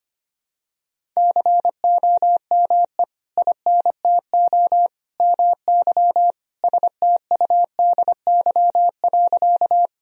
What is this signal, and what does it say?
Morse code, heard as a keyed tone, about 25 wpm, 700 Hz. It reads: COME INTO MY STUDY.
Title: Sentences from Top 1000 Words - Review Part 1 - 25wpm